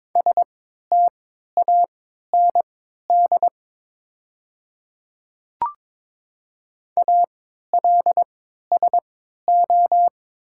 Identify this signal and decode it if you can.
Morse code, heard as a keyed tone, about 22 wpm, spaced as for 13 wpm, 700 Hz. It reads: STAND ALSO
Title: 3x Character Spacing - Top 300 Words - 22wpm